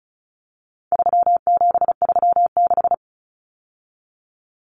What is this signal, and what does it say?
Morse code, heard as a keyed tone, about 34 wpm, 700 Hz. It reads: 3736
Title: Numbers - 4-Digit 35wpm